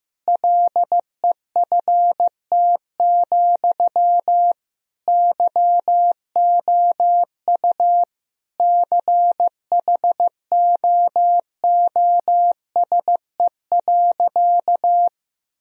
Morse code, heard as a keyed tone, about 16 wpm, 700 Hz. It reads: LEFT, YOU CHOOSE.
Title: Sentences from Top 1000 Words - Review Part 4 - 15wpm